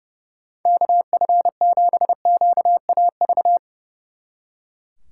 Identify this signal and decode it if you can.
Morse code, heard as a keyed tone, about 38 wpm, 700 Hz. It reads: KF7QAV